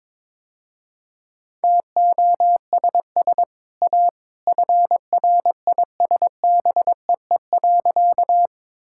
Morse code, heard as a keyed tone, about 21 wpm, 700 Hz. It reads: TOSS A FRISBEE.
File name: practice / 22wpm / Easy Sentences with Hints - 22wpm